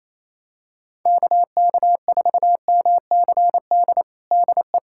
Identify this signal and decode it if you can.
Morse code, heard as a keyed tone, about 28 wpm, 700 Hz. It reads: KK4MCD DE
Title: USA Callsign DE Callsign - 6 Characters - 28wpm